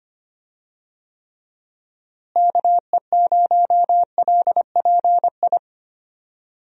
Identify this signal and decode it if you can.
Morse code, heard as a keyed tone, about 25 wpm, 700 Hz. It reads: KE0LPI
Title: USA Call Signs - 6 Characters - Repeated 3x - 25wpm